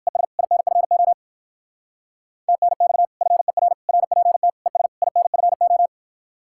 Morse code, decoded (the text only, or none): ES INFO TNX FER RPT ES INFO